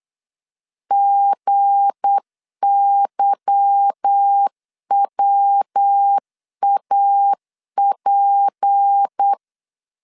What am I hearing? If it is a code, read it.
Morse code, heard as a keyed tone, about 8 wpm, 800 Hz. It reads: GYWAP
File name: LEZIONI / 8 YJ45 / 8 Y 8wpm 3245